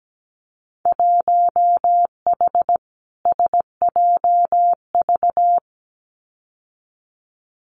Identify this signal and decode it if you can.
Morse code, heard as a keyed tone, about 17 wpm, 700 Hz. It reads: 1H SJV
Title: Winter Field Day Exchanges 17wpm